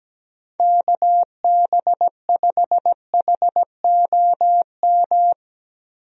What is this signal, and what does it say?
Morse code, heard as a keyed tone, about 17 wpm, 700 Hz. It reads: KB5HOM
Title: USA Call Signs - 6 Characters - Repeated 3x - 17wpm